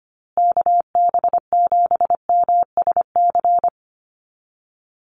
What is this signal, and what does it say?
Morse code, heard as a keyed tone, about 25 wpm, 700 Hz. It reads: KB7MSC
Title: USA Call Signs Repeated 2x 25wpm